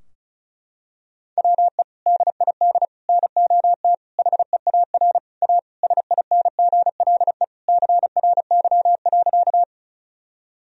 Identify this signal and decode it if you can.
Morse code, heard as a keyed tone, about 36 wpm, 700 Hz. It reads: WE DID NOT HEAR A SINGLE CRY.
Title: Sentences from Top 1000 Words - Review Part 4 - 35wpm